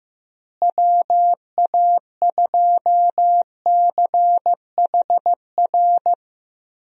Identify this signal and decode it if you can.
Morse code, heard as a keyed tone, about 15 wpm, 700 Hz. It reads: WA2CHR
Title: USA Call Signs - 6 Characters - Repeated 2x - 15wpm